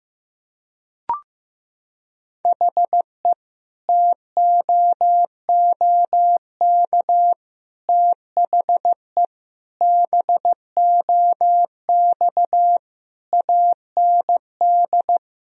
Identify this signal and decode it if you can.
Morse code, heard as a keyed tone, about 15 wpm, 700 Hz. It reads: HE TOOK THE BOX AND
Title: Sentences from Top 1000 Words - Review Part 2 - 15wpm